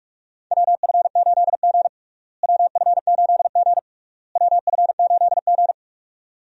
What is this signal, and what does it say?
Morse code, heard as a keyed tone, about 46 wpm, 700 Hz. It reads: WF8G WF8G WF8G
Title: USA Call Signs - 4 Characters - Repeated 3x - 45wpm